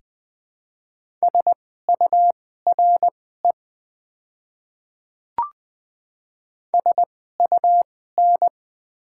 Morse code, heard as a keyed tone, about 20 wpm, spaced as for 14 wpm, 690 Hz. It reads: SURE SUN
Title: 2x Character Spacing - Top 300 Words - 20wpm